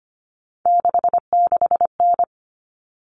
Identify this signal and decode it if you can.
Morse code, heard as a keyed tone, about 25 wpm, 700 Hz. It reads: 66N
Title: All Cut Numbers - 3-Digit 25wpm